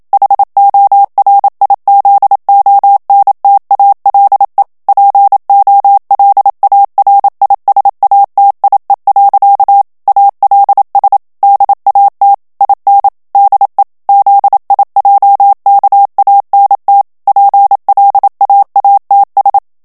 Morse code, heard as a keyed tone, about 27 wpm, 800 Hz. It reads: HORIZONTALE POLARISATIE. ALS DAT IN DE ZIJKANT PLAATS
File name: HOLLANDAIS texte clair 1 vitesse 26 mots minute